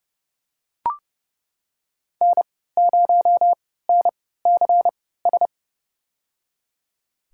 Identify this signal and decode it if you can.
Morse code, heard as a keyed tone, about 30 wpm, spaced as for 17 wpm, 700 Hz. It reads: N0NCS
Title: USA Call Signs - 5 Characters - 3x Character Spacing - 30wpm